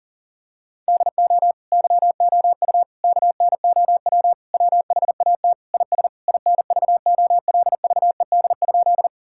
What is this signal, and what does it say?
Morse code, heard as a keyed tone, about 40 wpm, 700 Hz. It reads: DO YOU KNOW WHAT IS INVOLVED?
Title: Sentences from Top 1000 Words - Review Part 5 - 40wpm